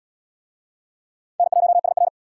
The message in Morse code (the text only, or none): N1SR